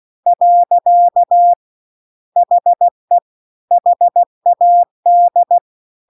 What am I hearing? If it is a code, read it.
Morse code, heard as a keyed tone, about 16 wpm, 700 Hz. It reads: . HE HAD
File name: CW-The-Big-Twoheaded-River-2-16WPM